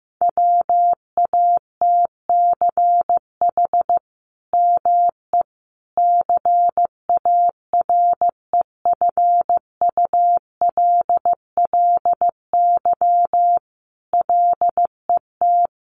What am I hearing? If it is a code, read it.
Morse code, heard as a keyed tone, about 15 wpm, 700 Hz. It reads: WATCH ME CAREFULLY LET
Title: Sentences from Top 1300 Words 15wpm